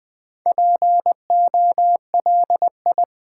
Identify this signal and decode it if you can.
Morse code, heard as a keyed tone, about 20 wpm, 700 Hz. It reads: POLI
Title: Build A Word - 2 Letters At A Time - 6-Letter Words - 20wpm